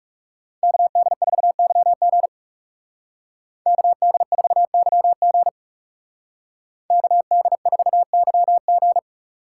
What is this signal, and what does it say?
Morse code, heard as a keyed tone, about 37 wpm, 700 Hz. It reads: KD4YG KD4YG KD4YG
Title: Speed Racing - USA 2x2 Call Signs 30wpm